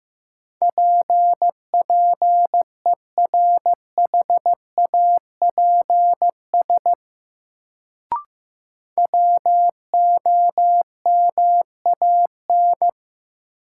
Morse code, heard as a keyed tone, about 15 wpm, 700 Hz. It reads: PPERHAPS WOMAN
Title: Sentences from Top 1100 Words - New Word Drill Alt - 15wpm